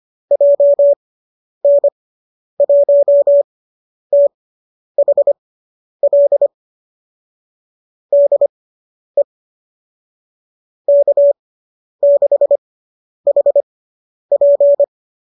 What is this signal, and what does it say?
Morse code, heard as a keyed tone, about 25 wpm, spaced as for 10 wpm, 590 Hz. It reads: JN1THL DE K6HP